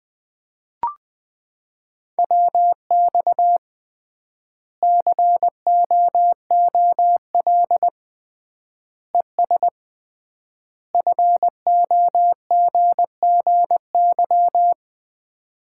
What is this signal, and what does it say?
Morse code, heard as a keyed tone, about 20 wpm, 700 Hz. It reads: WX COOL ES FOGGY
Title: Common QSO Phrases - 3x Word-Spacing - 20wpm